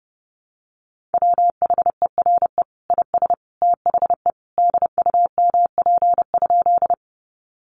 Morse code, heard as a keyed tone, about 28 wpm, 700 Hz. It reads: WHERE IS THE DUMP?
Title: Sentences from Top 1300 Words 30wpm